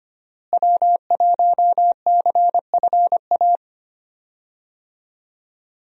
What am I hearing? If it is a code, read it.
Morse code, heard as a keyed tone, about 25 wpm, 700 Hz. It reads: W1CFA
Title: USA Call Signs - 5 Characters - Repeated 3x - 25wpm